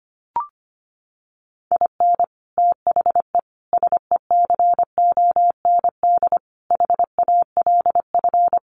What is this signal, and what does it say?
Morse code, heard as a keyed tone, about 25 wpm, 700 Hz. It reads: IN THE SECOND HALF
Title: Sets of 5 Words 25wpm